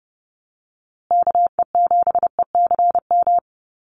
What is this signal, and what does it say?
Morse code, heard as a keyed tone, about 30 wpm, 700 Hz. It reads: KE7ECM
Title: USA Call Signs Repeated 2x 30wpm